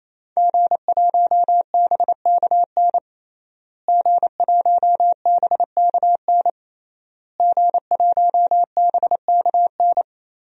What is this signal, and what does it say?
Morse code, heard as a keyed tone, about 28 wpm, 700 Hz. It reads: G1BKN G1BKN G1BKN
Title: DXCC-100 Call Signs - 5 Characters - Repeated 3x - 28wpm